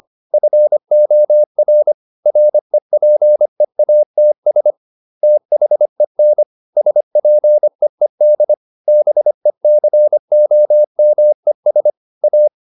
Morse code, heard as a keyed tone, about 25 wpm, 600 Hz. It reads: FOR REPEATS THEN SPEED BECOMES A